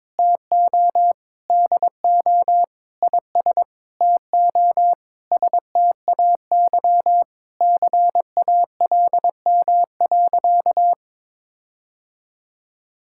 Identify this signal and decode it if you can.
Morse code, heard as a keyed tone, about 22 wpm, 700 Hz. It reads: TO DO IS TO STAY CALM.
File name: Sentences with 7-Word Phrases - 22wpm